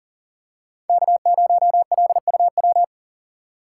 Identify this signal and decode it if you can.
Morse code, heard as a keyed tone, about 40 wpm, 690 Hz. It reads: K0LUW
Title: K1USN SST QSO Practice - 40wpm